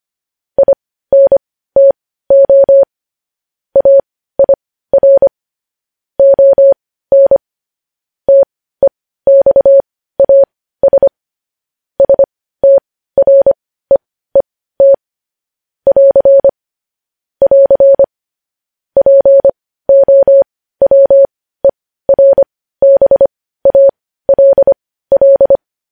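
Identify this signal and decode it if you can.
Morse code, heard as a keyed tone, about 25 wpm, spaced as for 15 wpm, 550 Hz. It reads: INTO AIR ON TEXAS STREET <AR> <AR> POWERBALL
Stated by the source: Morse code news headlines at 15 wpm.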